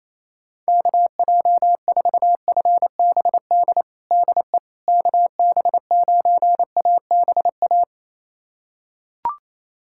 Morse code, heard as a keyed tone, about 28 wpm, 700 Hz. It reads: KJ4FBD DE KB9ABA
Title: USA Callsign DE Callsign - 6 Characters - 28wpm